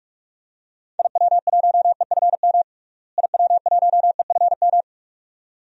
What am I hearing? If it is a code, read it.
Morse code, heard as a keyed tone, about 43 wpm, 700 Hz. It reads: IW1EFM IW1EFM
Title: DXCC-100 Call Signs - 6 Characters - Repeated 2x - 45wpm